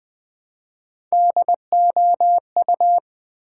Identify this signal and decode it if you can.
Morse code, heard as a keyed tone, about 20 wpm, 700 Hz. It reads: DOU
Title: Build A Word - 1 Letter At A Time - 3 to 5-Letter Words - 20wpm